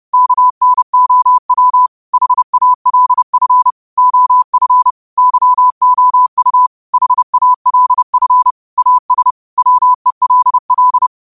Morse code, heard as a keyed tone, about 30 wpm, 1000 Hz. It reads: KNOW HALF OF YOU HALF AS WELL